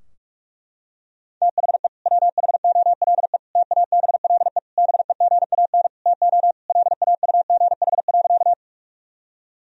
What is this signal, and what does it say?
Morse code, heard as a keyed tone, about 46 wpm, 700 Hz. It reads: THE WHOLE TABLE BEGAN TO LAUGH.
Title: Sentences from Top 1000 Words - Review Part 2 - 45wpm